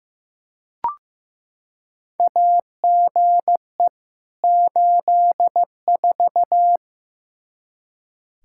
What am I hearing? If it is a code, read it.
Morse code, heard as a keyed tone, about 15 wpm, 700 Hz. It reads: AGE 84
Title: Practice QSO - Exchange 3 - More Info - 15wpm